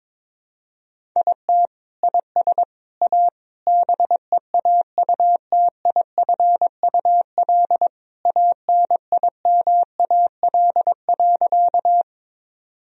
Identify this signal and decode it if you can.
Morse code, heard as a keyed tone, about 22 wpm, 700 Hz. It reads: IT IS A BEAUTIFUL ANIMAL.
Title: Sentences from Top 1000 Words - Review Part 1 - 22wpm